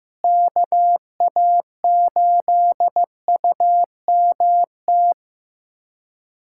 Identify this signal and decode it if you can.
Morse code, heard as a keyed tone, about 15 wpm, 700 Hz. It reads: KA8UMT
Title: USA Callsign DE Callsign - 6 Characters - 15wpm